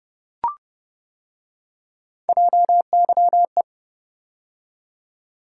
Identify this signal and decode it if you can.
Morse code, heard as a keyed tone, about 29 wpm, 700 Hz. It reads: JYE